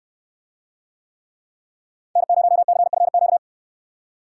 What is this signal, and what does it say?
Morse code, heard as a keyed tone, about 69 wpm, 700 Hz. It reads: N0CFQ